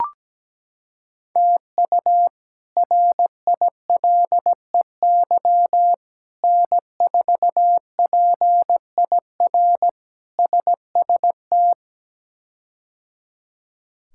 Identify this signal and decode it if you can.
Morse code, heard as a keyed tone, about 17 wpm, 700 Hz. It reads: TU RILEY N4PIR SST
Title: K1USN SST QSO Practice - 17wpm